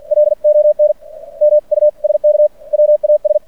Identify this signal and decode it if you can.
Morse code, heard as a keyed tone, about 50 wpm, spaced as for 43 wpm, 590 Hz. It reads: LON MUSK WAS
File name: archived / CRSnet